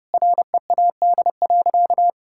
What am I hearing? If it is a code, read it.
Morse code, heard as a keyed tone, about 30 wpm, 700 Hz. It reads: READ.
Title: Sentences from Top 1000 Words - Review Part 5 - 30wpm